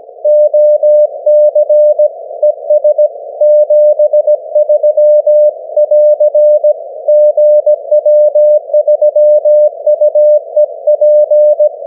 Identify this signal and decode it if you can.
Morse code, heard as a keyed tone, about 17 wpm, 600 Hz. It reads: OC ES 73<AR> GW3UEP
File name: GW3UEP / 16Dec12 9AM G3XIZ 501kHz